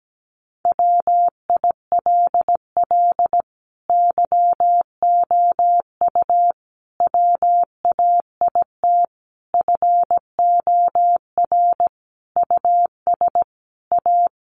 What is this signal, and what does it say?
Morse code, heard as a keyed tone, about 17 wpm, 700 Hz. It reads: WILL YOU WAIT FOR US A
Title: Sentences from Top 900 Words 17wpm